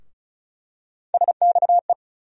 Morse code, heard as a keyed tone, about 34 wpm, 700 Hz. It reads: SXE
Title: USA Call Sign Postfixes - 3 Letters 35wpm